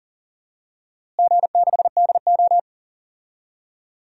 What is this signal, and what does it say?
Morse code, heard as a keyed tone, about 40 wpm, 700 Hz. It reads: G6DO
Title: DXCC-100 Call Signs - 4 Characters - Repeated 3x - 40wpm